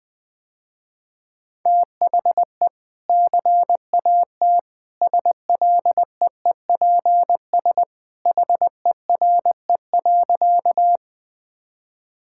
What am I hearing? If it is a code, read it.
Morse code, heard as a keyed tone, about 21 wpm, 700 Hz. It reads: THE CAT SLEEPS HERE.
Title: Easy Sentences with Hints - 20wpm